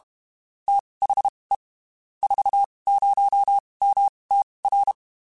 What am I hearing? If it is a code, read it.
Morse code, heard as a keyed tone, about 32 wpm, spaced as for 23 wpm, 800 Hz. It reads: THE 40MTR
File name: (spaced) / QSOs Spaced-mp3s / kc7haa @30 sp